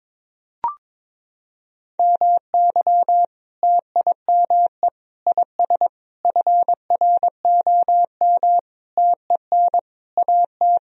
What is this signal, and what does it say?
Morse code, heard as a keyed tone, about 22 wpm, 700 Hz. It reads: MY TIME IS FROM TEN AT
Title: Sentences from Top 1000 Words - Review Part 2 - 22wpm